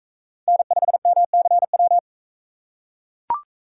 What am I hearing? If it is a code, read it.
Morse code, heard as a keyed tone, about 41 wpm, 700 Hz. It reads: N5MCW